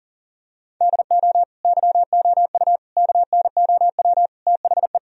Morse code, heard as a keyed tone, about 40 wpm, 700 Hz. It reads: DO YOU KNOW THE
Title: Sentences from Top 1000 Words - Review Part 1 - 40wpm